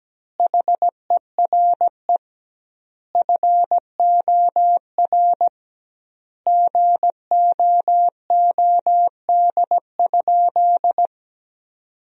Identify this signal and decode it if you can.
Morse code, heard as a keyed tone, about 17 wpm, 700 Hz. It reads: HERE FOR GOOD?
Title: 2x Word-Spacing - Sentences from Top 200 Words - 17wpm